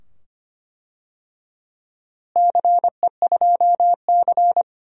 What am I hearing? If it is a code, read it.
Morse code, heard as a keyed tone, about 25 wpm, 700 Hz. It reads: CE2C
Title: DXCC-100 Call Signs - 4 Characters - Repeated 2x - 25wpm